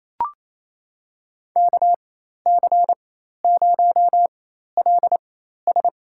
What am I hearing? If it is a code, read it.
Morse code, heard as a keyed tone, about 27 wpm, spaced as for 13 wpm, 700 Hz. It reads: KC0LS
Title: USA Call Signs - 4x Character Spacing - 28wpm